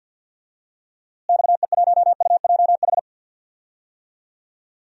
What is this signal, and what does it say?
Morse code, heard as a keyed tone, about 47 wpm, 700 Hz. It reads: XE1UJH
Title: DXCC-100 Call Signs - 6 Characters - Repeated 3x - 50wpm